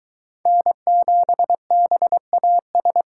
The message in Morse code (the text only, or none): N7BAS